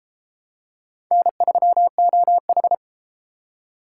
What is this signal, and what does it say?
Morse code, heard as a keyed tone, about 30 wpm, 700 Hz. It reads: N3OH